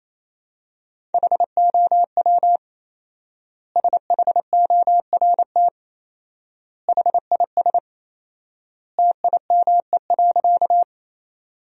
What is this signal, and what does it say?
Morse code, heard as a keyed tone, about 27 wpm, 700 Hz. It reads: HOW SHORT HIS TIME.
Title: 4x Word-Spacing - Sentences from Top 400 Words - 28wpm